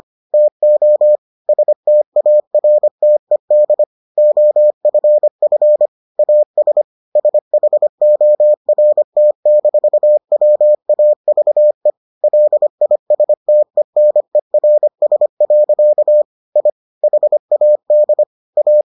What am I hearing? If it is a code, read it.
Morse code, heard as a keyed tone, about 25 wpm, 600 Hz. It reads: TO STARTED OFF AS SHORT-WAVE LISTENERS. I HAD A